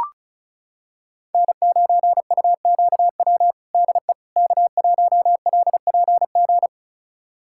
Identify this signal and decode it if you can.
Morse code, heard as a keyed tone, about 34 wpm, 690 Hz. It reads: N9UQW DE K1LPG
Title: USA Callsign DE Callsign - 5 Characters - 35wpm